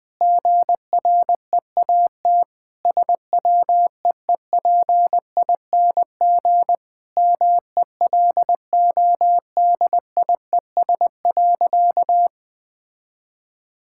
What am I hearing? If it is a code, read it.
Morse code, heard as a keyed tone, about 20 wpm, 700 Hz. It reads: GREAT SWEEPING MELODIES.
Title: All-Age Appropriate Jokes - 20wpm